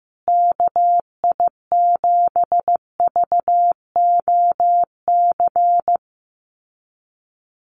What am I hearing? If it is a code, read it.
Morse code, heard as a keyed tone, about 15 wpm, 700 Hz. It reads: KI7VOC
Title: USA Call Signs Repeated 2x 15wpm